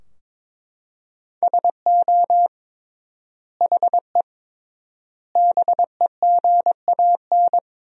Morse code, heard as a keyed tone, about 21 wpm, 700 Hz. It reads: SO HE BEGAN
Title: 3x Word-Spacing - Sentences from Top 300 Words - 22wpm